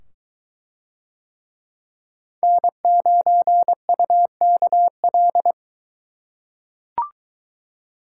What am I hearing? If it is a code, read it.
Morse code, heard as a keyed tone, about 22 wpm, 700 Hz. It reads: N9UKL